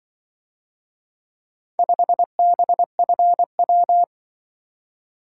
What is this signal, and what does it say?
Morse code, heard as a keyed tone, about 24 wpm, 700 Hz. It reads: 5BFW
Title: Speed-Racing - Four Letter-Number Combinations - 20wpm